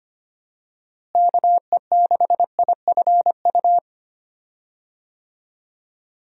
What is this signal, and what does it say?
Morse code, heard as a keyed tone, about 25 wpm, 700 Hz. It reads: KE6IFU
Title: USA Call Signs - 6 Characters - Repeated 2x - 25wpm